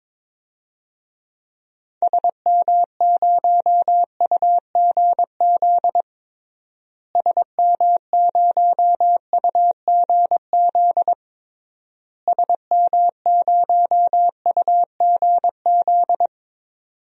Morse code, heard as a keyed tone, about 22 wpm, 700 Hz. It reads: SM0UGZ SM0UGZ SM0UGZ
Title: DXCC-100 Call Signs - 6 Characters - Repeated 3x - 22wpm